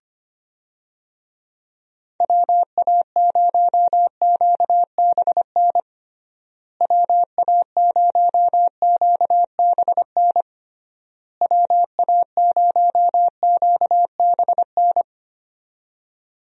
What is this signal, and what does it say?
Morse code, heard as a keyed tone, about 25 wpm, 700 Hz. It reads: WA0QBN WA0QBN WA0QBN
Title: USA Call Signs - 6 Characters - Repeated 3x - 25wpm